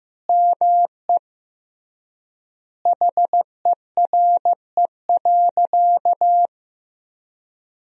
Morse code, heard as a keyed tone, about 15 wpm, 700 Hz. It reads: ME HERE.
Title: 3x Word-Spacing - Sentences from Top 200 Words - 15wpm